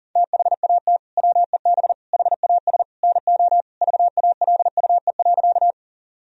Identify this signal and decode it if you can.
Morse code, heard as a keyed tone, about 40 wpm, 700 Hz. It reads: THAT WEB HAS NO VALUE.
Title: Sentences from Top 1000 Words - Review Part 4 - 40wpm